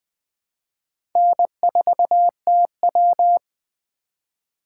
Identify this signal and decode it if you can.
Morse code, heard as a keyed tone, about 24 wpm, 700 Hz. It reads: N4TW
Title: Speed Racing - USA 1x2 Call Signs 20wpm